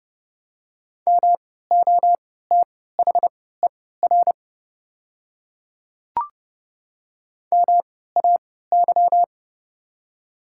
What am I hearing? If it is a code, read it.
Morse code, heard as a keyed tone, about 30 wpm, spaced as for 17 wpm, 700 Hz. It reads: MOTHER MAY
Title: 3x Character Spacing - Top 200 Words - 30wpm